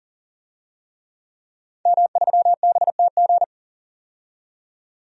Morse code, heard as a keyed tone, about 40 wpm, 700 Hz. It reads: M3BTG
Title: DXCC-100 Call Signs - 5 Characters - 40wpm